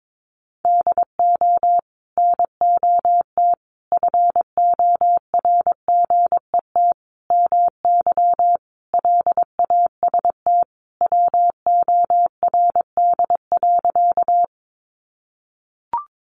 Morse code, heard as a keyed tone, about 22 wpm, 700 Hz. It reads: DO NOT FORGET MY LAST WORD.
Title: Sentences from Top 1200 Words 22wpm